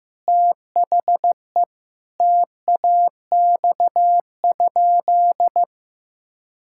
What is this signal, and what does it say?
Morse code, heard as a keyed tone, about 15 wpm, 700 Hz. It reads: THE TAX?
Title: Sentences from Top 1000 Words - Review Part 5 - 15wpm